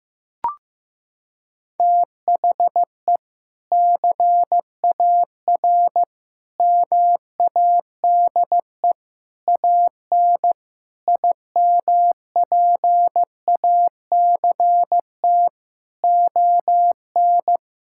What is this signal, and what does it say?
Morse code, heard as a keyed tone, about 16 wpm, 700 Hz. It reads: THE CAR MADE AN IMPACT ON
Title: Sentences from Top 1000 Words - Review Part 6 - 15wpm